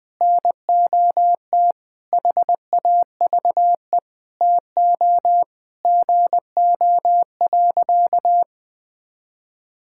Morse code, heard as a keyed tone, about 20 wpm, 700 Hz. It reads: NOT HAVE TO GO.
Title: Sentences with 4-Word Phrases - Spoken Beginning - 20wpm